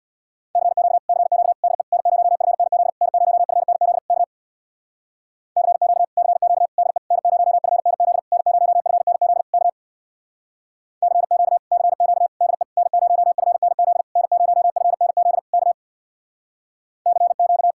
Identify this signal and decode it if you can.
Morse code, heard as a keyed tone, about 64 wpm, 700 Hz. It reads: CQ CQ DE N0FNZ N0FNZ K CQ CQ DE N0FNZ N0FNZ K CQ CQ DE N0FNZ N0FNZ K CQ